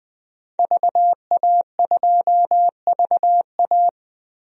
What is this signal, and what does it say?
Morse code, heard as a keyed tone, about 20 wpm, 700 Hz. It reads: VA2VA
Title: DXCC-100 Call Signs - 5 Characters - 20wpm